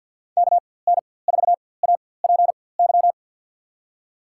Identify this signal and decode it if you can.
Morse code, heard as a keyed tone, about 50 wpm, spaced as for 23 wpm, 700 Hz. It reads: KN4APY
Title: USA Call Signs - 6 Characters - 4x Character Spacing - 50wpm